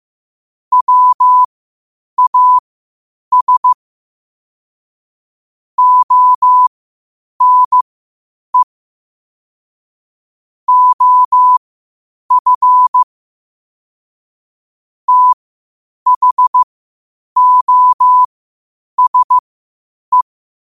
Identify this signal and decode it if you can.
Morse code, heard as a keyed tone, about 15 wpm, spaced as for 8 wpm, 1000 Hz. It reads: WAS ONE OF THOSE
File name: qotdfiles / 8wpm